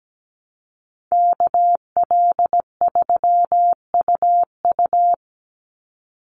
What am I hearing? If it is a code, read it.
Morse code, heard as a keyed tone, about 17 wpm, 700 Hz. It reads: KL3UU
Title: USA Call Signs 17wpm